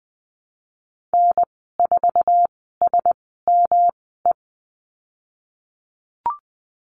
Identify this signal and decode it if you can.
Morse code, heard as a keyed tone, about 20 wpm, spaced as for 15 wpm, 700 Hz. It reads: N4SME
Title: 154 Lesson 25g - Letter p - Call Signs 20wpm